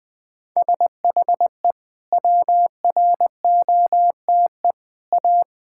Morse code, heard as a keyed tone, about 21 wpm, 700 Hz. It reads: SHE WROTE A
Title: Sentences with 2-Word Phrases - 20wpm